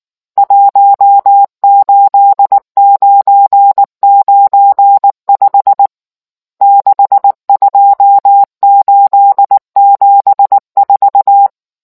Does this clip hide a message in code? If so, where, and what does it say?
Yes: Morse code from 0.4 to 11.5 s, heard as a keyed tone, about 19 wpm, 800 Hz. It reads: 18995 62874